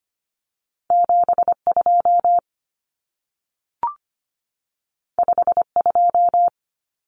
Morse code, heard as a keyed tone, about 25 wpm, 700 Hz. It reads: 72 52
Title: Common Cut Numbers - 2-Digit 25wpm